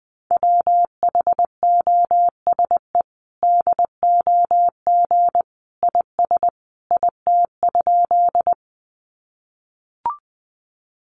Morse code, heard as a keyed tone, about 20 wpm, 700 Hz. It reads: WHOSE DOG IS IT?
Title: Sentences from Top 1200 Words 20wpm